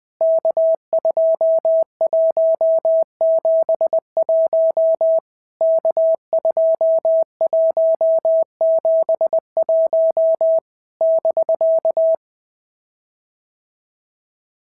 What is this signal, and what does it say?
Morse code, heard as a keyed tone, about 20 wpm, 650 Hz. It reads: K2171 K2171 <BK>
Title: POTA QSO Practice - 20wpm